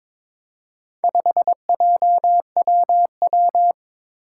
Morse code, heard as a keyed tone, about 22 wpm, 700 Hz. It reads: 5JWW
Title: Four Challenging Letter-Number Combinations 22wpm